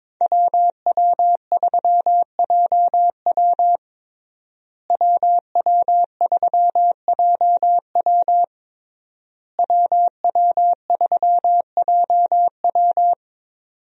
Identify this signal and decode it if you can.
Morse code, heard as a keyed tone, about 22 wpm, 690 Hz. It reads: WW3JW WW3JW WW3JW
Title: USA Call Signs - 5 Characters - Repeated 3x - 22wpm